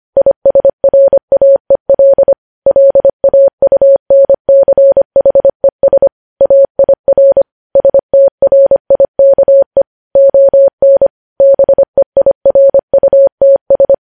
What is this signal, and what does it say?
Morse code, heard as a keyed tone, about 25 wpm, 550 Hz. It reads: ISRAEL LAUNCHES AIR STRIKE ON BEIRUTS
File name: podcast / News headlines-1x-25x25